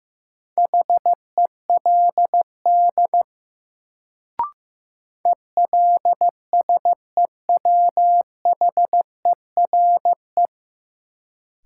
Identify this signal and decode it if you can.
Morse code, heard as a keyed tone, about 15 wpm, 700 Hz. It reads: HELD ELSEWHERE
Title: Sentences from Top 1600 Words - New Word Drill - 15wpm